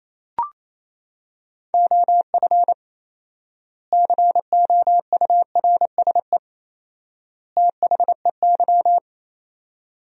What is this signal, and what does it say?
Morse code, heard as a keyed tone, about 28 wpm, 700 Hz. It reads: OF COURSE THEY
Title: 4x Word-Spacing - Sentences from Top 500 Words - 28wpm